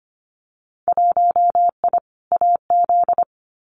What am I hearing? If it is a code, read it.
Morse code, heard as a keyed tone, about 25 wpm, 690 Hz. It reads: 1I AZ
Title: Winter Field Day Exchanges 25wpm